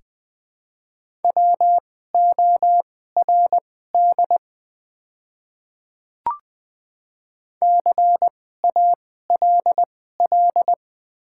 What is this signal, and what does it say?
Morse code, heard as a keyed tone, about 20 wpm, spaced as for 15 wpm, 690 Hz. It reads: WORD CALL
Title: 2x Character Spacing - Top 100 Words - 20wpm